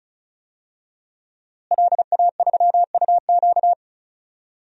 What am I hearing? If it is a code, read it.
Morse code, heard as a keyed tone, about 36 wpm, 700 Hz. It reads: LA3UQ
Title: DXCC-100 Call Signs - 5 Characters - Repeated 2x - 35wpm